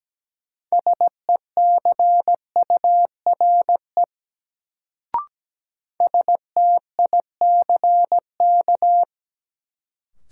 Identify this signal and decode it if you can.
Morse code, heard as a keyed tone, about 17 wpm, 700 Hz. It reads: SECURE STICK
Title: Sentences from Top 1200 Words - New Word Drill - 17wpm